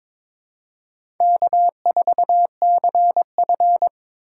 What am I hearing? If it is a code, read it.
Morse code, heard as a keyed tone, about 22 wpm, 700 Hz. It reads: K4CF
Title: USA Call Signs - 4 Characters - 8x Character Spacing - 22wpm